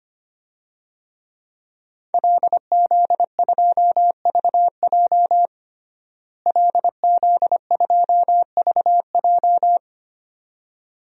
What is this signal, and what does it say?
Morse code, heard as a keyed tone, about 25 wpm, 700 Hz. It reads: LZ2VJ LZ2VJ
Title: DXCC-100 Call Signs - 5 Characters - Repeated 2x - 25wpm